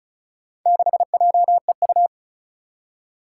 Morse code, heard as a keyed tone, about 36 wpm, 690 Hz. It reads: 6JEU
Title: Four Challenging Letter-Number Combinations 35wpm